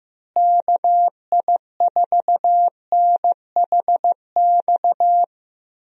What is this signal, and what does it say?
Morse code, heard as a keyed tone, about 15 wpm, 700 Hz. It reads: KI4NHX
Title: USA Call Signs - 6 Characters - Repeated 2x - 15wpm